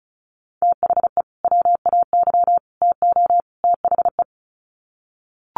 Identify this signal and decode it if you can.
Morse code, heard as a keyed tone, about 34 wpm, 700 Hz. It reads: THE WAY TO THE
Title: Sets of 4 Words 35wpm